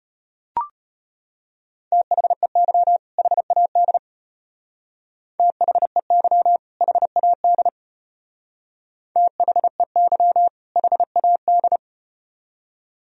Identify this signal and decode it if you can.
Morse code, heard as a keyed tone, about 35 wpm, 700 Hz. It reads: THEY HAD THEY HAD THEY HAD